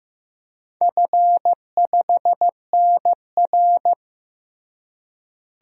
Morse code, heard as a keyed tone, about 14 wpm, 700 Hz. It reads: F5NR
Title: DXCC-100 Call Signs - 4 Characters - Repeated 3x - 15wpm